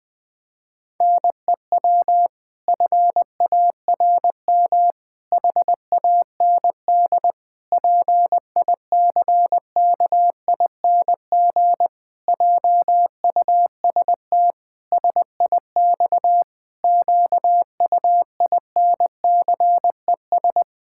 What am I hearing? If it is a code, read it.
Morse code, heard as a keyed tone, about 20 wpm, 700 Hz. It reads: NEW FARM HAND PICKING JUST SIX QUINCES
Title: Pangram Challenge v2 - 20wpm